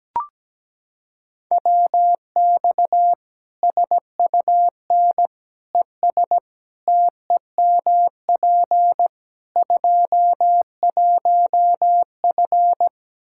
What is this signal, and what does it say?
Morse code, heard as a keyed tone, about 17 wpm, 700 Hz. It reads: WX SUN ES TEMP 21F
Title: Practice QSO - Exchange 2 - Info - 17wpm